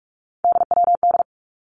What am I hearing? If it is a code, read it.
Morse code, heard as a keyed tone, about 46 wpm, 700 Hz. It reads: DWD
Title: Three Challenging Letter-Number Combinations 45wpm